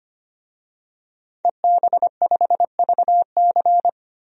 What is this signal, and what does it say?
Morse code, heard as a keyed tone, about 25 wpm, 700 Hz. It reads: EB5VC
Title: DXCC-100 Call Signs - 5 Characters - 25wpm